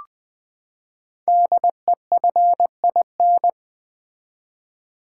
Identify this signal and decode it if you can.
Morse code, heard as a keyed tone, about 20 wpm, 700 Hz. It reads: DEFIN
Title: Build A Word - 5 Letters At A Time - 10-Letter Words - 20wpm